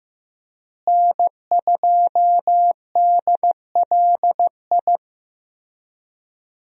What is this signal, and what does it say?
Morse code, heard as a keyed tone, about 15 wpm, 700 Hz. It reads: N2DLI
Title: USA Call Signs - 5 Characters - Repeated 2x - 15wpm